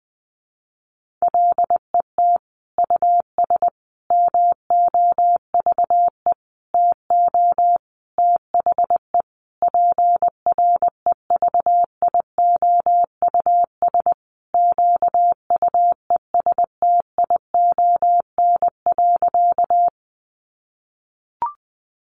Morse code, heard as a keyed tone, about 20 wpm, 700 Hz. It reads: LET US MOVE TO THE PREVIOUS QUESTION.
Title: Sentences from Top 1500 Words 20wpm